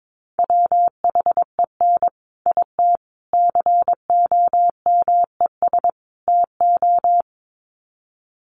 Morse code, heard as a keyed tone, about 22 wpm, 700 Hz. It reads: WHEN IT COMES TO
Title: Sets of 4 Words 22wpm